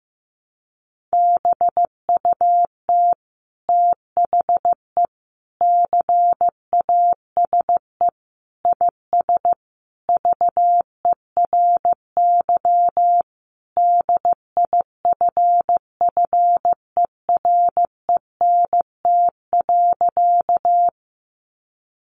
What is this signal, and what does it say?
Morse code, heard as a keyed tone, about 15 wpm, 700 Hz. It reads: BUT THE CASE IS VERY DIFFERENT.
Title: Sentences from Top 700 Words 15wpm